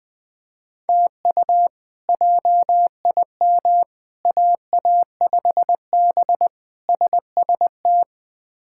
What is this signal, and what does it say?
Morse code, heard as a keyed tone, about 20 wpm, 700 Hz. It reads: TU JIM AA5B SST
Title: K1USN SST QSO Practice - 20wpm